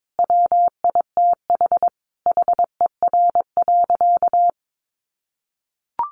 Morse code, heard as a keyed tone, about 22 wpm, 700 Hz. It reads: WITH HER.
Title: Sentences from Top 1100 Words 22wpm